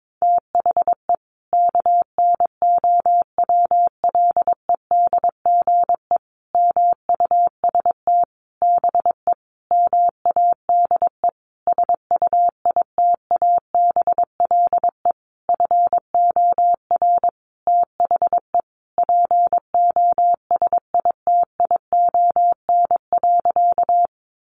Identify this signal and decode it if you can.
Morse code, heard as a keyed tone, about 22 wpm, 700 Hz. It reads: THE KNOWLEDGE MUST BE MADE SUITABLE FOR THE POSITION.
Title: Sentences from Top 1400 Words 22wpm